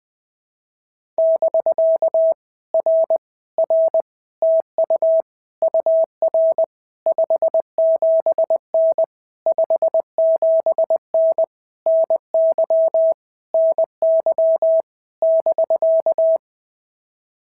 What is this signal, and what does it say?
Morse code, heard as a keyed tone, about 20 wpm, 650 Hz. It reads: <BK> R R TU UR 57N 57N NY NY <BK>
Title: POTA QSO Practice - 20wpm